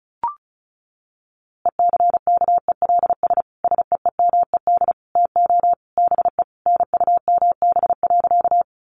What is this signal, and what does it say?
Morse code, heard as a keyed tone, about 34 wpm, 700 Hz. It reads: ECKELS SEEMED TO BE NUMB.
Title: A Sound Of Thunder 35wpm